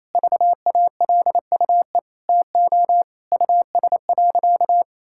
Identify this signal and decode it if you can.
Morse code, heard as a keyed tone, about 27 wpm, 700 Hz. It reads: VALUE TO US.
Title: Sentences from Top 1000 Words - Review Part 5 - 28wpm